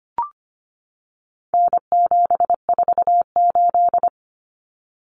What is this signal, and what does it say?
Morse code, heard as a keyed tone, about 25 wpm, 700 Hz. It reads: N748
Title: Common Cut Numbers - 4-Digit 25wpm